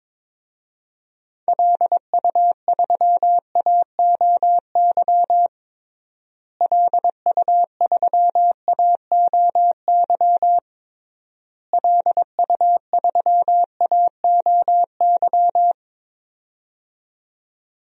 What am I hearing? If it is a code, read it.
Morse code, heard as a keyed tone, about 22 wpm, 700 Hz. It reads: LU3AOY LU3AOY LU3AOY
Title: DXCC-100 Call Signs - 6 Characters - Repeated 3x - 22wpm